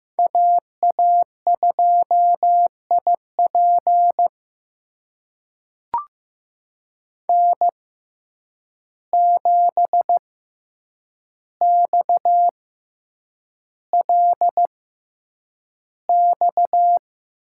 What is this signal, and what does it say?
Morse code, heard as a keyed tone, about 15 wpm, spaced as for 8 wpm, 700 Hz. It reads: AA2IP N7XLX
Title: USA Call Signs - 5 Characters - 6x Character Spacing - 15wpm